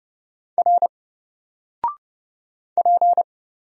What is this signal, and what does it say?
Morse code, heard as a keyed tone, about 30 wpm, 700 Hz. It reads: R P
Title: Single Character (Letter:Number:Punctuation:Prosign) 30wpm